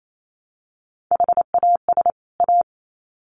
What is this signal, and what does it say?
Morse code, heard as a keyed tone, about 28 wpm, 700 Hz. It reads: HAS A
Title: Sets of 2 Words 28wpm